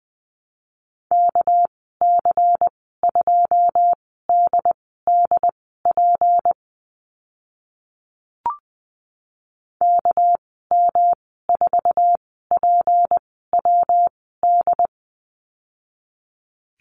Morse code, heard as a keyed tone, about 20 wpm, spaced as for 15 wpm, 700 Hz. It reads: KC2DDP KM4PWD
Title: 219 Lesson 35g - Number 0 - Call Signs 20wpm